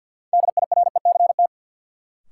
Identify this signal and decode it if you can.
Morse code, heard as a keyed tone, about 53 wpm, 700 Hz. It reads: DIRECT
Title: Sentences from Top 400 Words - New Word Drill - 50wpm